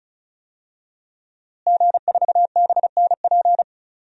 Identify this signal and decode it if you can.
Morse code, heard as a keyed tone, about 36 wpm, 700 Hz. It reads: G4BNP
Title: DXCC-100 Call Signs - 5 Characters - Repeated 3x - 35wpm